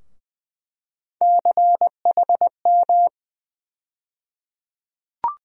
Morse code, heard as a keyed tone, about 20 wpm, 700 Hz. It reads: CHM
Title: USA Call Sign Postfixes - 3 Letters 20wpm